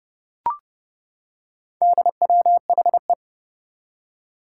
Morse code, heard as a keyed tone, about 29 wpm, 700 Hz. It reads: DWHE
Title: Four Challenging Letter-Number Combinations 30wpm